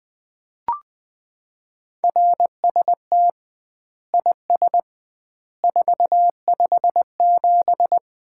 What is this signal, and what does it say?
Morse code, heard as a keyed tone, about 20 wpm, 700 Hz. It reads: RST IS 457
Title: Common QSO Phrases - 2x Word-Spacing - 20wpm